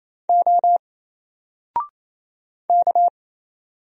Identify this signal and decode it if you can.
Morse code, heard as a keyed tone, about 27 wpm, 700 Hz. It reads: O K
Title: Single Letter-Number 28wpm